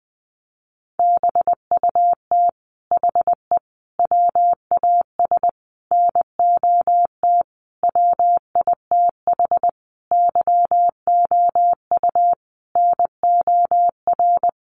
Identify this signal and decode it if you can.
Morse code, heard as a keyed tone, about 20 wpm, 700 Hz. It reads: BUT HE WAS NOT WITH YOU NOR
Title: Sentences from Top 1500 Words 20wpm